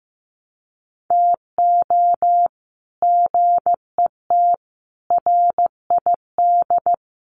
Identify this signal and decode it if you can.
Morse code, heard as a keyed tone, about 15 wpm, 700 Hz. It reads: TO GET RID
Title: Sets of 5 Words 15wpm